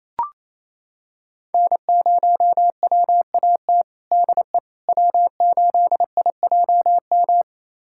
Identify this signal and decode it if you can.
Morse code, heard as a keyed tone, about 28 wpm, 700 Hz. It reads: N0WAT DE W8IJM
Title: USA Callsign DE Callsign - 5 Characters - 28wpm